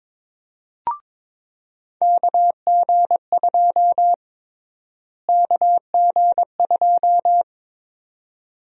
Morse code, heard as a keyed tone, about 22 wpm, 700 Hz. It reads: KG2 KG2
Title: USA Call Sign Prefixes - Repeated 2x - 22wpm